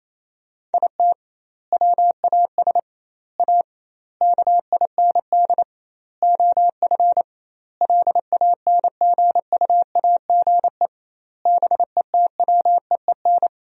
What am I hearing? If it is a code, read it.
Morse code, heard as a keyed tone, about 28 wpm, 700 Hz. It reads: IT WAS A KIND OF LANGUAGE BETWEEN
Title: 2x Word-Spacing - Sentences from Top 500 Words - 28wpm